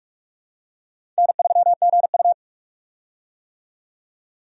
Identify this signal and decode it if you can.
Morse code, heard as a keyed tone, about 46 wpm, 700 Hz. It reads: N3GU
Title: USA Call Signs - 4 Characters - 45wpm